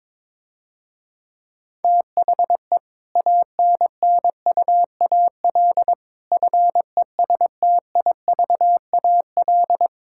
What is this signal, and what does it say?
Morse code, heard as a keyed tone, about 22 wpm, 700 Hz. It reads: THE ANNUAL FESTIVAL
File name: Difficult Sentences with Hints - 22wpm